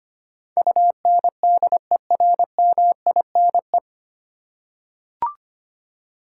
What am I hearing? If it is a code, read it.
Morse code, heard as a keyed tone, about 25 wpm, 700 Hz. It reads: UNDERMINE
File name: Nine Letter Words - 25wpm